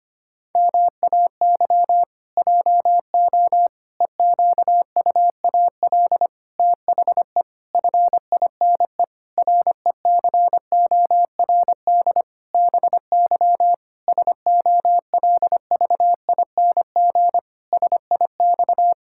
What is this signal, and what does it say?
Morse code, heard as a keyed tone, about 25 wpm, 700 Hz. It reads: MAY JO EQUAL THE FINE RECORD BY SOLVING SIX
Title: Pangram Challenge v2 - 25wpm